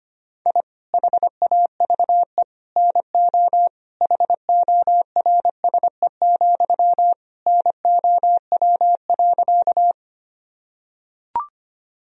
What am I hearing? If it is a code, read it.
Morse code, heard as a keyed tone, about 25 wpm, 700 Hz. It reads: I HAVE NO HORSE, NOW.
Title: Sentences from Top 1000 Words - Review Part 1 - 25wpm